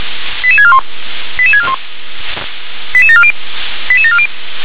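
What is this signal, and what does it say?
Five-tone selective call (ZVEI), heard as quick runs of five tones, tones of about 70 ms. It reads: 88641 88641 88740 88740
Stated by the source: SDR receiver recording of ZVEI five-tone selective call 88851.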